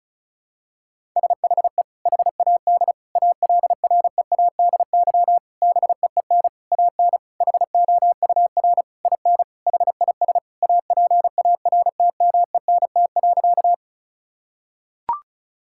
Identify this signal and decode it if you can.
Morse code, heard as a keyed tone, about 36 wpm, 700 Hz. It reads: SHE HAD ALREADY BEEN AN HOUR IN HIS APARTMENT.
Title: Sentences from Top 1000 Words - Review Part 6 - 35wpm